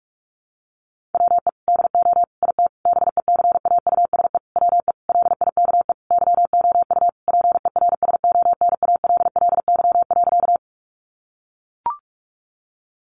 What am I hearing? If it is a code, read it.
Morse code, heard as a keyed tone, about 44 wpm, 700 Hz. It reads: WE DO IT BECAUSE WE LIKE YOU PERSONALLY.
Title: Sentences from Top 1400 Words 45wpm